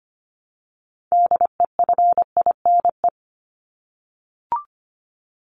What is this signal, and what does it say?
Morse code, heard as a keyed tone, about 25 wpm, 700 Hz. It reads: DEFINE
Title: ED Suffix - Top 100 Words with Base Words 25wpm